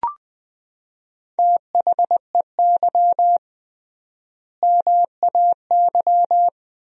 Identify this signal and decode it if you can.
Morse code, heard as a keyed tone, about 20 wpm, 700 Hz. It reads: THEY MAY
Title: 3x Word-Spacing - Sentences from Top 300 Words - 20wpm